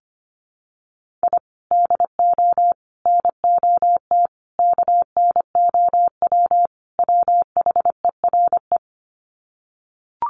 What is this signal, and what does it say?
Morse code, heard as a keyed tone, about 25 wpm, 700 Hz. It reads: I DO NOT KNOW WHERE
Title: Sets of 5 Words 25wpm